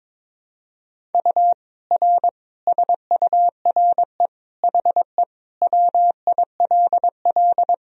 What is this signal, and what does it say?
Morse code, heard as a keyed tone, about 21 wpm, 700 Hz. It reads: U R SURE HE WILL
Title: Sentences from Top 300 Words and Abbreviations 22wpm